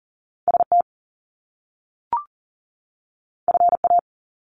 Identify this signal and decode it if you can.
Morse code, heard as a keyed tone, about 40 wpm, 700 Hz. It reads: ST FA
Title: Top Two Letter Combinations in Words 40wpm